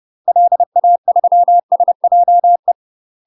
Morse code, heard as a keyed tone, about 30 wpm, 700 Hz. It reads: LA3SJE
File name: UK MASTER QSOs TRAINING / 30WPM / UK MASTER QSOs 0001